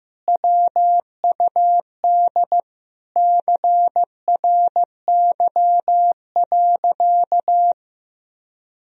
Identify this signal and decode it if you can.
Morse code, heard as a keyed tone, about 15 wpm, 700 Hz. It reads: WUD CRY.
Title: Sentences from Top 500 Words and Abbreviations 15wpm